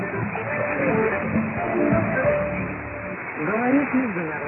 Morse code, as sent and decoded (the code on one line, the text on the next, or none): none